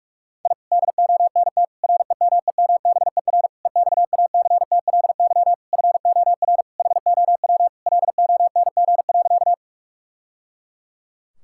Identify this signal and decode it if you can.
Morse code, heard as a keyed tone, about 46 wpm, 700 Hz. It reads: I DONT REMEMBER EXACTLY FOR HOW LONG.
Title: Sentences from Top 1000 Words - Review Part 6 - 45wpm